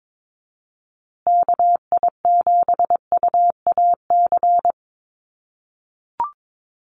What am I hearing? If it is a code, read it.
Morse code, heard as a keyed tone, about 22 wpm, 700 Hz. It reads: KI7UAC
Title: USA Call Signs Repeated 3x 22wpm